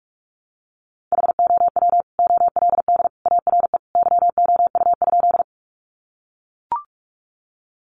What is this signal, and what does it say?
Morse code, heard as a keyed tone, about 44 wpm, 700 Hz. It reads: HOW OLD ARE YOU?
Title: Sentences from Top 200 Words 45wpm